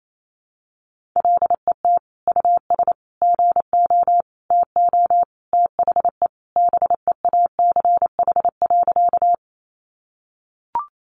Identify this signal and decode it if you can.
Morse code, heard as a keyed tone, about 28 wpm, 700 Hz. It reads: LET US GO TO THE BEACH.
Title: Sentences from Top 1500 Words 28wpm